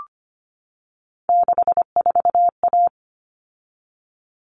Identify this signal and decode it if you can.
Morse code, heard as a keyed tone, about 25 wpm, 700 Hz. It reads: 64A
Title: All Cut Numbers - 3-Digit 25wpm